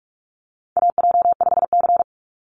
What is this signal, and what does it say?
Morse code, heard as a keyed tone, about 44 wpm, 690 Hz. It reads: AJ5C
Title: NA SOTA Chaser Callsigns 45wpm